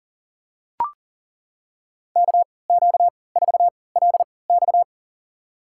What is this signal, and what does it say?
Morse code, heard as a keyed tone, about 40 wpm, spaced as for 23 wpm, 700 Hz. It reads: KQ4LX
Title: USA Call Signs - 5 Characters - 3x Character Spacing - 40wpm